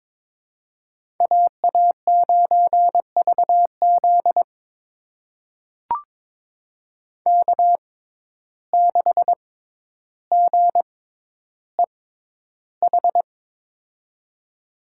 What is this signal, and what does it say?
Morse code, heard as a keyed tone, about 22 wpm, spaced as for 11 wpm, 700 Hz. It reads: AA9VZ K6GEH
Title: USA Call Signs - 5 Characters - 6x Character Spacing - 22wpm